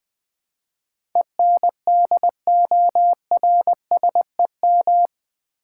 Morse code, heard as a keyed tone, about 20 wpm, 700 Hz. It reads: ENDORSEM
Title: Build A Word - 4 Letters At A Time - 11-Letter Words - 20wpm